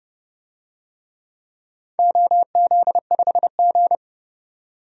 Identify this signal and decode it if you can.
Morse code, heard as a keyed tone, about 31 wpm, 700 Hz. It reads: OZ5G
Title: DXCC-100 Call Signs - 4 Characters - 30wpm